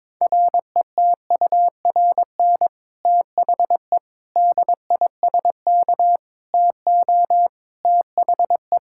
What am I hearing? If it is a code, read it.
Morse code, heard as a keyed tone, about 22 wpm, 700 Hz. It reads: RETURN THE DISK TO THE
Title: Sentences from Top 1000 Words - Review Part 6 - 22wpm